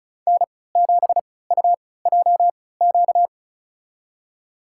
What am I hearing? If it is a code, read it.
Morse code, heard as a keyed tone, about 34 wpm, spaced as for 20 wpm, 700 Hz. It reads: N7UJQ
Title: USA Call Signs - 3x Character Spacing - 35wpm